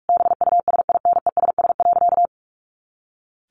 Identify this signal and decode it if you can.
Morse code, heard as a keyed tone, about 44 wpm, 700 Hz. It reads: BUSINESS.
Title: A Sound Of Thunder 45wpm